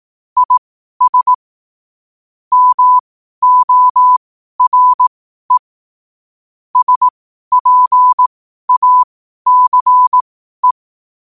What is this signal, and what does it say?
Morse code, heard as a keyed tone, about 18 wpm, spaced as for 12 wpm, 1000 Hz. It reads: IS MORE SPACE
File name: qotdfiles / 12wpm